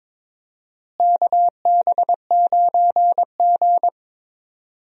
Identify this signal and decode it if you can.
Morse code, heard as a keyed tone, about 22 wpm, 700 Hz. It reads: KB9G
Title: USA Call Signs - 4 Characters - Repeated 2x - 22wpm